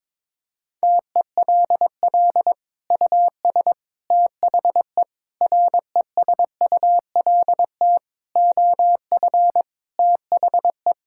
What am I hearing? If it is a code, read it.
Morse code, heard as a keyed tone, about 22 wpm, 700 Hz. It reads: TELL US THE RESULT OF THE
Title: Sentences from Top 1600 Words 22wpm